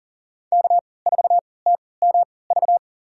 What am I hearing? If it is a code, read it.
Morse code, heard as a keyed tone, about 40 wpm, spaced as for 23 wpm, 700 Hz. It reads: K4TMV
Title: USA Call Signs - 5 Characters - 3x Character Spacing - 40wpm